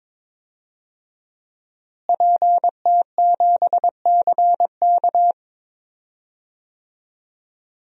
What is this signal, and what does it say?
Morse code, heard as a keyed tone, about 22 wpm, 700 Hz. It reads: PT7CK
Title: DXCC-100 Call Signs - 5 Characters - Repeated 2x - 22wpm